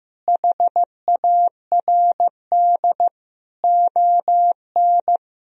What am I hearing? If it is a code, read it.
Morse code, heard as a keyed tone, about 15 wpm, 700 Hz. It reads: HARD ON
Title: Sentences from Top 1000 Words - Review Part 1 - 15wpm